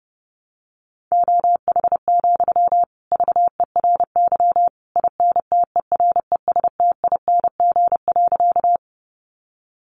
Morse code, heard as a keyed tone, about 30 wpm, 700 Hz. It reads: OH, VERY INTERESTING.
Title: Sentences from Top 1000 Words 30wpm